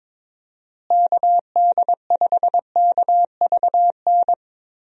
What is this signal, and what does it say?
Morse code, heard as a keyed tone, about 22 wpm, 700 Hz. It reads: KD5KVN
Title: USA Call Signs - 6 Characters - Repeated 3x - 22wpm